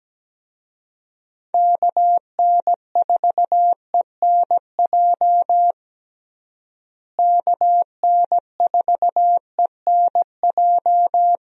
Morse code, heard as a keyed tone, about 17 wpm, 700 Hz. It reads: KN4ENJ KN4ENJ
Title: USA Call Signs - 6 Characters - Repeated 2x - 17wpm